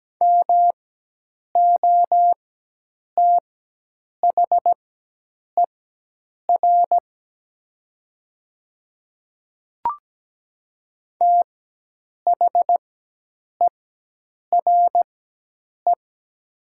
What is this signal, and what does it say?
Morse code, heard as a keyed tone, about 17 wpm, spaced as for 8 wpm, 700 Hz. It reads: MOTHER THERE
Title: 4x Character Spacing - Top 200 Words - 17wpm